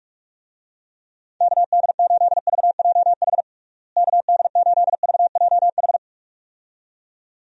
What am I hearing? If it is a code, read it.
Morse code, heard as a keyed tone, about 46 wpm, 700 Hz. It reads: KD8VJH KD8VJH
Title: USA Call Signs - 6 Characters - Repeated 2x - 45wpm